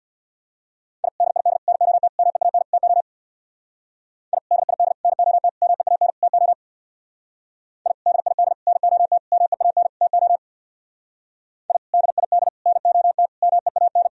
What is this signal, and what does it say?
Morse code, heard as a keyed tone, about 59 wpm, 690 Hz. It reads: I DID NOT MEAN TO I DID NOT MEAN TO I DID NOT MEAN TO I DID NOT MEAN
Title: Speed-Racing - Sets of 5 Words 50wpm